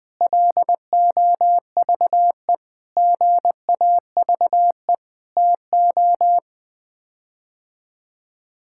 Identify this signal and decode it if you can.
Morse code, heard as a keyed tone, about 20 wpm, 700 Hz. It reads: LOVE GAVE TO
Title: Sets of 8 Words - 20wpm